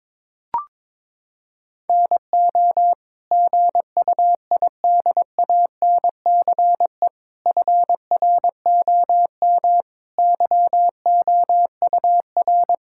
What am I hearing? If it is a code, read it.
Morse code, heard as a keyed tone, about 22 wpm, 700 Hz. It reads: NO GUIDANCE FROM YOUR
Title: Sentences from Top 1600 Words 22wpm